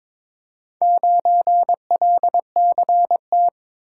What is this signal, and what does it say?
Morse code, heard as a keyed tone, about 23 wpm, 700 Hz. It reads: 9LCT
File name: practice / 22wpm / Four Letter-Number Combinations - 22wpm